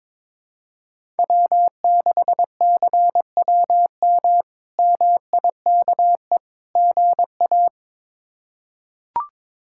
Morse code, heard as a keyed tone, about 22 wpm, 700 Hz. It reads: W6CWM MIKE GA
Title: K1USN SST QSO Practice - 22wpm